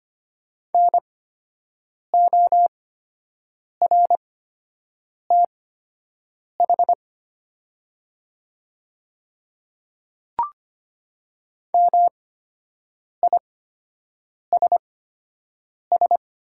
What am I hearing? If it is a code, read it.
Morse code, heard as a keyed tone, about 25 wpm, spaced as for 7 wpm, 690 Hz. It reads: NORTH MISS
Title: 8x Character Spacing - Top 500 Words - 25wpm